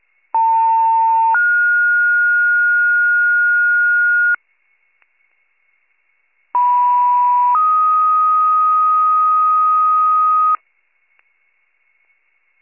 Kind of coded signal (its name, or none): two-tone page